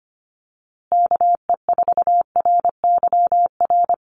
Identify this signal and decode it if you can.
Morse code, heard as a keyed tone, about 25 wpm, 700 Hz. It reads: KE4RYR
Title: USA Call Signs Repeated 3x 25wpm